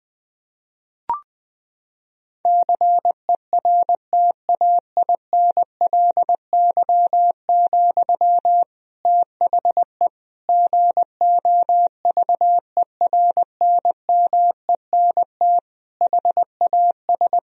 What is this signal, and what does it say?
Morse code, heard as a keyed tone, about 20 wpm, 700 Hz. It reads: CERTAINLY, THE GOVERNMENT HAS
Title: Sentences from Top 1000 Words - Review Part 3 - 20wpm